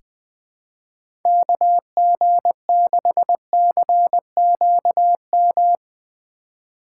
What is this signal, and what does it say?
Morse code, heard as a keyed tone, about 20 wpm, 700 Hz. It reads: KG6CQM
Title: USA Call Signs - 6 Characters - 4x Character Spacing - 20wpm